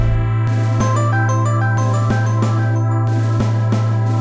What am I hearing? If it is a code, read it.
There is no code.